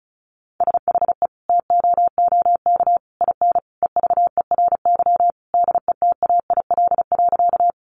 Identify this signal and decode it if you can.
Morse code, heard as a keyed tone, about 34 wpm, 700 Hz. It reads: SHE TOOK IN EVERY DETAIL.
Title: Sentences from Top 1500 Words 35wpm